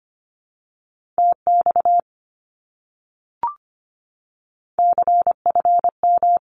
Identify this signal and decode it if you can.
Morse code, heard as a keyed tone, about 25 wpm, 700 Hz. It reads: TX CFM
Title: Abbreviations Repeated 3x - 25wpm